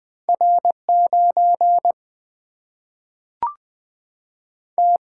Morse code, heard as a keyed tone, about 20 wpm, 700 Hz. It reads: R 9 T
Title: Single-Letter Single-Number 20wpm